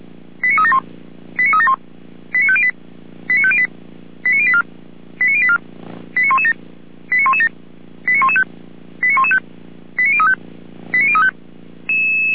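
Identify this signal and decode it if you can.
Five-tone selective call (ZVEI), heard as quick runs of five tones, tones of about 70 ms. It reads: 89371 89371 89508 89508 89084 89084 89117 89117 89115 89115 89035 89035
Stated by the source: ZVEI five-tone selective call 89120, received by SDR.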